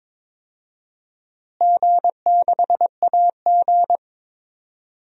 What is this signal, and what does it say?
Morse code, heard as a keyed tone, about 22 wpm, 700 Hz. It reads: G6AG
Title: DXCC-100 Call Signs - 4 Characters - Repeated 3x - 22wpm